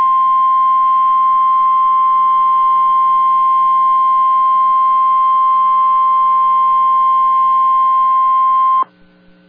Emergency Alert System (SAME) tones (the 1050 Hz attention tone). It is heard from 1.1 to 8.8 s.